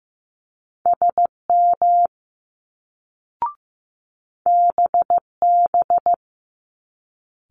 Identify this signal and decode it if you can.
Morse code, heard as a keyed tone, about 15 wpm, 700 Hz. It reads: SM BB
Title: Multi-letter Phonemes Quick Fire 15wpm